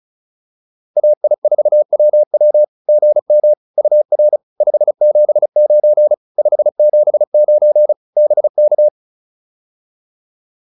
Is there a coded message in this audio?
Yes: Morse code from 1.0 to 8.9 s, heard as a keyed tone, about 35 wpm, 600 Hz. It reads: AI4WW GM UR 579 579 BK